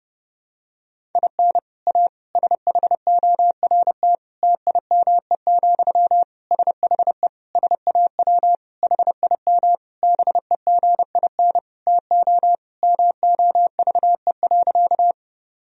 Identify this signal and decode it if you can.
Morse code, heard as a keyed tone, about 28 wpm, 700 Hz. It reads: IN A SHORT TIME, SHE SAW HIM BEGIN TO MOVE.
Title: Sentences from Top 1000 Words - Review Part 2 - 30wpm